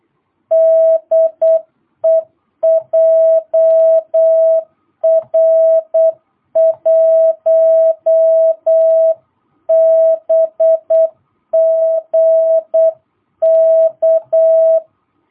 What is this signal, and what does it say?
Morse code, heard as a keyed tone, about 8 wpm, 650 Hz. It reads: DE JR1BGK
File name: jh8cbh / 2022cw